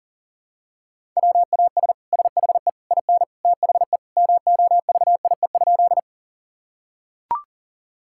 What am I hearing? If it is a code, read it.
Morse code, heard as a keyed tone, about 40 wpm, 700 Hz. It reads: WAS SHE IN THE MOVIE?
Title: Sentences from Top 1000 Words - Review Part 5 - 40wpm